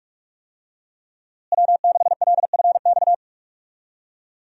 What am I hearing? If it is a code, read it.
Morse code, heard as a keyed tone, about 41 wpm, 700 Hz. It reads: W6LFX
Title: USA Call Signs - 5 Characters - Repeated 2x - 45wpm